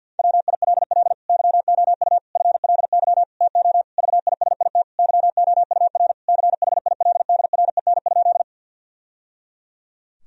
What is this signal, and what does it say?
Morse code, heard as a keyed tone, about 50 wpm, 700 Hz. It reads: WILL YOU FLY TO VISIT YOUR CHILDREN?
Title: Sentences from Top 1000 Words - Review Part 2 - 50wpm